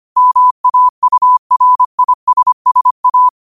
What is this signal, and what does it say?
Morse code, heard as a keyed tone, about 25 wpm, 1000 Hz. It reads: MAURISSA
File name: qotdfiles / 25wpm